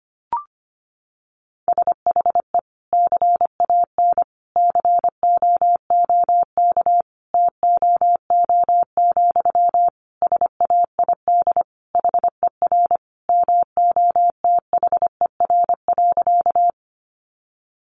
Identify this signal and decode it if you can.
Morse code, heard as a keyed tone, about 25 wpm, 700 Hz. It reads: SHE CAN COOK TOO, SAID HER MOTHER.
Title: Sentences from Top 700 Words 25wpm